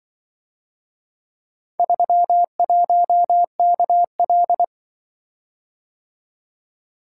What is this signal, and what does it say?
Morse code, heard as a keyed tone, about 26 wpm, 700 Hz. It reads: 31KL